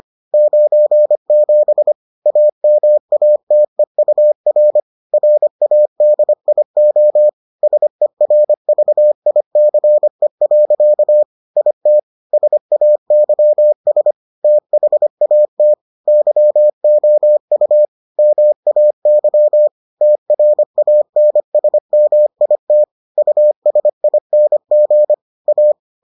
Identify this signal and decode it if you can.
Morse code, heard as a keyed tone, about 25 wpm, 600 Hz. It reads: 97 AMATEUR RADIO SERVICE. IT SAYS THAT YOU MAY TRANSMIT USING A